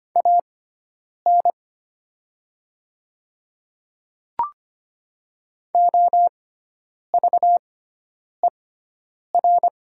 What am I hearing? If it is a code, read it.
Morse code, heard as a keyed tone, about 25 wpm, spaced as for 9 wpm, 700 Hz. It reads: AN OVER
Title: 6x Character Spacing - Top 100 Words - 25wpm